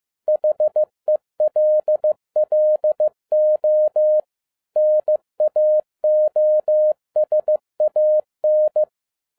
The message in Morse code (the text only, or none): HELLO NAOSAN